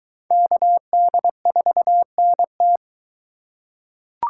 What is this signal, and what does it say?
Morse code, heard as a keyed tone, about 21 wpm, 700 Hz. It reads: KD4NT